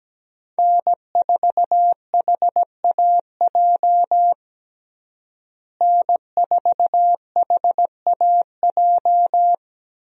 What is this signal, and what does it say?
Morse code, heard as a keyed tone, about 17 wpm, 700 Hz. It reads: N4HAJ N4HAJ
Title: USA Call Signs - 5 Characters - Repeated 3x - 17wpm